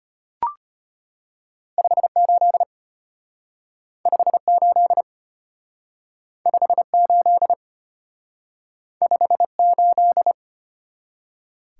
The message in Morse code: 58 58 58 58

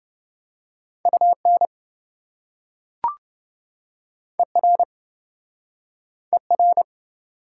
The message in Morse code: UN ER ER

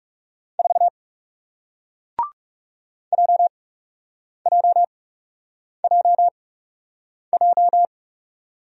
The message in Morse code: 4 JJJJ

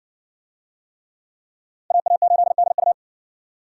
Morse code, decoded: AA8DU

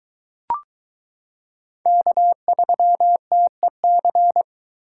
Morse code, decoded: K3TEC

